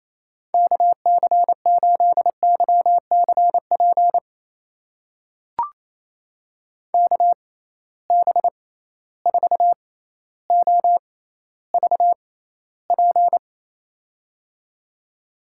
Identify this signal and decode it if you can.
Morse code, heard as a keyed tone, about 28 wpm, spaced as for 14 wpm, 700 Hz. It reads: KC8YCP KB4OVP
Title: USA Call Signs - 6 Characters - 6x Character Spacing - 28wpm